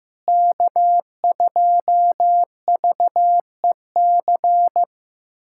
Morse code, heard as a keyed tone, about 16 wpm, 700 Hz. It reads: K2VEC
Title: USA Callsign DE Callsign - 5 Characters - 15wpm